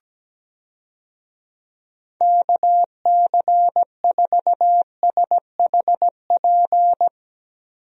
Morse code, heard as a keyed tone, about 17 wpm, 700 Hz. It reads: KC4SHP
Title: USA Call Signs - 6 Characters - Repeated 3x - 17wpm